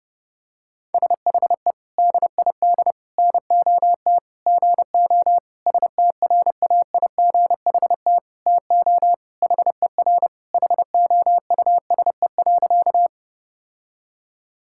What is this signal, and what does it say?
Morse code, heard as a keyed tone, about 30 wpm, 700 Hz. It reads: SHE DID NOT GO STRAIGHT TO HER HOUSE.
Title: Sentences from Top 1000 Words - Review Part 6 - 30wpm